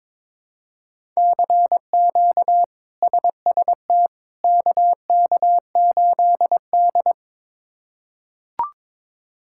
Morse code, heard as a keyed tone, about 22 wpm, 700 Hz. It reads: CQ SST KK8D
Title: K1USN SST QSO Practice - 22wpm